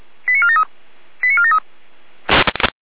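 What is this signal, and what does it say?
Five-tone selective call (ZVEI), heard as quick runs of five tones, tones of about 70 ms. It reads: 89473 89473
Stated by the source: ZVEI five-tone call received on an SDR.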